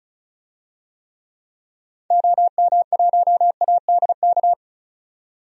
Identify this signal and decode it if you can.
Morse code, heard as a keyed tone, about 36 wpm, 700 Hz. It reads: OM1ADK
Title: DXCC-100 Call Signs - 6 Characters - 35wpm